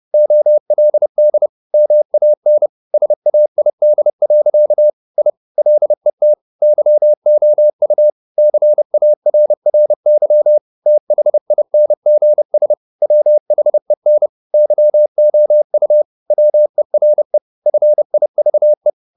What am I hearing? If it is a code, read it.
Morse code, heard as a keyed tone, about 30 wpm, 600 Hz. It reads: OLD MAN SAID. I LET YOU CARRY THINGS WHEN YOU WERE FIVE
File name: the old man and the sea / the old man and the sea 1 30wpm